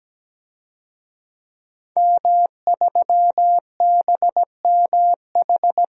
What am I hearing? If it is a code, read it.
Morse code, heard as a keyed tone, about 17 wpm, 700 Hz. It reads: M3BMH
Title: DXCC-100 Call Signs - 5 Characters - 17wpm